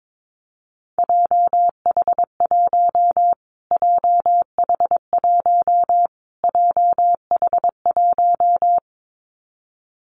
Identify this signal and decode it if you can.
Morse code, heard as a keyed tone, about 22 wpm, 700 Hz. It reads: JH1 JH1 JH1
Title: Callsign Prefixes for Top 25 DXCC - Repeated 3x - 22wpm